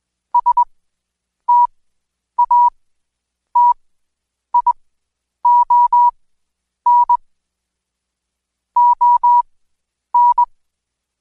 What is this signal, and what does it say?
Morse code, heard as a keyed tone, about 21 wpm, spaced as for 9 wpm, 970 Hz. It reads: STATION ON
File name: Code Quick / AudioDownld2 / Seg115-Day20-DayDrill 20